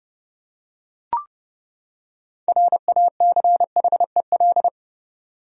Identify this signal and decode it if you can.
Morse code, heard as a keyed tone, about 30 wpm, 700 Hz. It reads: RACHEL
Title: USA Common Names - 30wpm